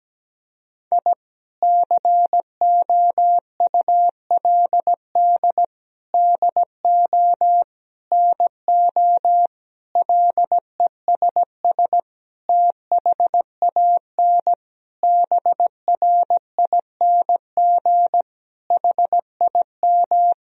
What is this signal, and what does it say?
Morse code, heard as a keyed tone, about 17 wpm, 700 Hz. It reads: I COULD DO NO LESS THAN BRING HIM
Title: Sentences from Top 1000 Words - Review Part 3 - 17wpm